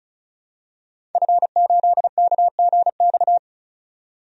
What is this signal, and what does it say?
Morse code, heard as a keyed tone, about 36 wpm, 700 Hz. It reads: F8KGX
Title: DXCC-100 Call Signs - 5 Characters - 35wpm